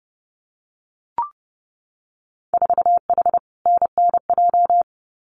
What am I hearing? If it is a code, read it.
Morse code, heard as a keyed tone, about 30 wpm, 700 Hz. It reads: 4H NNJ
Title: Winter Field Day Exchanges Encore 30wpm